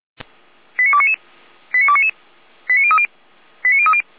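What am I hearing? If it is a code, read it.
Five-tone selective call (ZVEI), heard as quick runs of five tones, tones of about 70 ms. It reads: 89299 89299 89030 89030
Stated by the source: Five-tone ZVEI call 89337 received by SDR, recorded by Feuerwehr Orlamünde.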